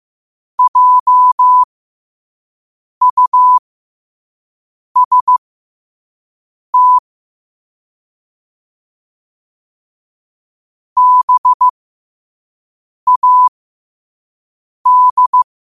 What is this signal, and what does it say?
Morse code, heard as a keyed tone, about 15 wpm, spaced as for 5 wpm, 1000 Hz. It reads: JUST BAD
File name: 5wpm